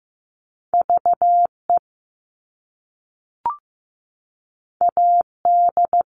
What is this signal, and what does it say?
Morse code, heard as a keyed tone, about 15 wpm, 700 Hz. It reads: VE AD
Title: All Cut Numbers - 2-Digit 15wpm